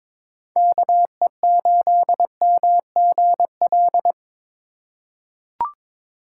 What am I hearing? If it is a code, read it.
Morse code, heard as a keyed tone, about 23 wpm, 700 Hz. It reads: KE8MGL
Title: USA Call Signs - 2x Character Spacing - 22wpm